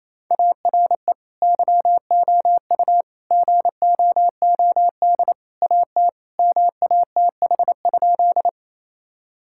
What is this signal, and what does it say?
Morse code, heard as a keyed tone, about 28 wpm, 700 Hz. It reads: ARE YOU GOOD AT MATH?
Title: Sentences from Top 1000 Words - Review Part 6 - 28wpm